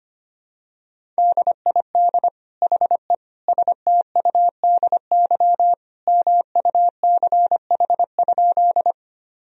Morse code, heard as a keyed tone, about 25 wpm, 700 Hz. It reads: DID HE STUDY MUCH?
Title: Sentences from Top 1000 Words - Review Part 1 - 25wpm